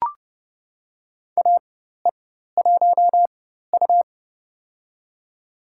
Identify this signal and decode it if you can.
Morse code, heard as a keyed tone, about 31 wpm, spaced as for 14 wpm, 700 Hz. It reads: AE1U
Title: USA Call Signs - 4x Character Spacing - 30wpm